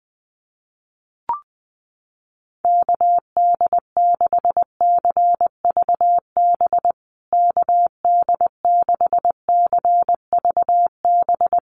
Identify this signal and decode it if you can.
Morse code, heard as a keyed tone, about 20 wpm, 700 Hz. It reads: KD6CVB KD6CVB
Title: USA Call Signs Repeated 3x 20wpm